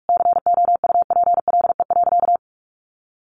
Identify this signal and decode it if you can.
Morse code, heard as a keyed tone, about 44 wpm, 700 Hz. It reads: COUPLE.
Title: Sentences from Top 1200 Words 45wpm